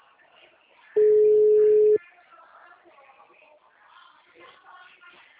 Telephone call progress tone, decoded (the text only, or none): ringback tone